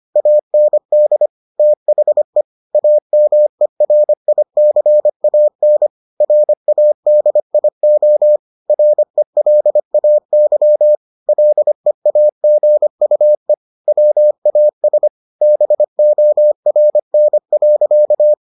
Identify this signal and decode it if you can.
Morse code, heard as a keyed tone, about 25 wpm, 600 Hz. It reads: AND THE AMERICAN RADIO RELAY LEAGUE WAS BORN.